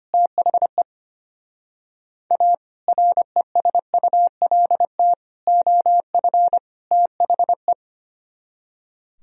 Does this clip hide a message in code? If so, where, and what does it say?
Yes: Morse code from 0.1 to 7.7 s, heard as a keyed tone, about 33 wpm, 700 Hz. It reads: THE A RESULT OF THE